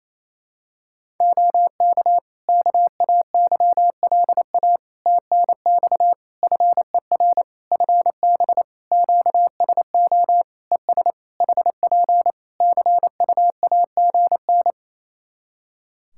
Morse code, heard as a keyed tone, about 28 wpm, 700 Hz. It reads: OK KAYLA TNX FER FB QSO ES HP CUAGN
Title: Practice QSO - Exchange 4 - Ending - 28wpm